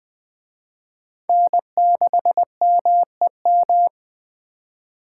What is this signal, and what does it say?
Morse code, heard as a keyed tone, about 19 wpm, 700 Hz. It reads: N6MEM